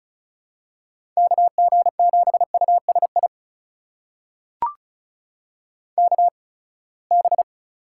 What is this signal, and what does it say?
Morse code, heard as a keyed tone, about 36 wpm, spaced as for 24 wpm, 700 Hz. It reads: KG7USI KB
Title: USA Call Signs - 6 Characters - 8x Character Spacing - 35wpm